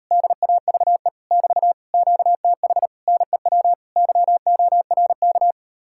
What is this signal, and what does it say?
Morse code, heard as a keyed tone, about 38 wpm, 700 Hz. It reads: DAVE <BT> QTH NEW YORK